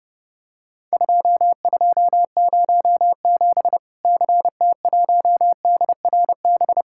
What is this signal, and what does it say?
Morse code, heard as a keyed tone, about 30 wpm, 700 Hz. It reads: 2207 CT1DRB
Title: Worked All Europe DX Contest - QTC Exchange - 30wpm